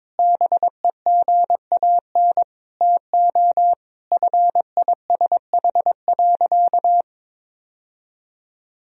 Morse code, heard as a keyed tone, about 22 wpm, 700 Hz. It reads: BEGAN TO FISH.
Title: Sentences from Top 1000 Words - Review Part 1 - 22wpm